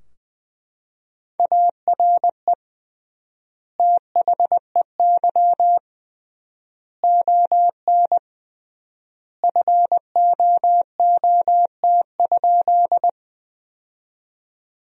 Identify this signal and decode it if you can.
Morse code, heard as a keyed tone, about 20 wpm, 700 Hz. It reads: ARE THEY ON FOOT?
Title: 3x Word-Spacing - Sentences from Top 500 Words - 20wpm